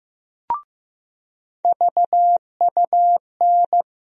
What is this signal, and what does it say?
Morse code, heard as a keyed tone, about 15 wpm, 700 Hz. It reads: VUN
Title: USA Call Sign Postfixes - 3 Letters 15wpm